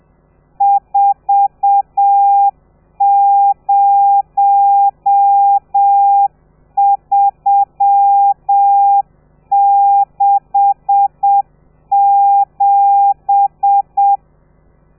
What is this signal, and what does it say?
Morse code, heard as a keyed tone, about 7 wpm, 800 Hz. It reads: 40367
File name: telegrafia / i2tfj / Lez13 30